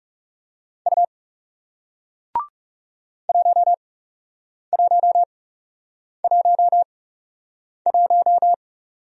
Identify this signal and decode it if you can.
Morse code, heard as a keyed tone, about 40 wpm, spaced as for 8 wpm, 700 Hz. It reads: U 1111